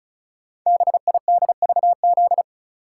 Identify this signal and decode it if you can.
Morse code, heard as a keyed tone, about 36 wpm, 700 Hz. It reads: BIDVZ
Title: Code Groups - 5 Letter:Numbers - 35wpm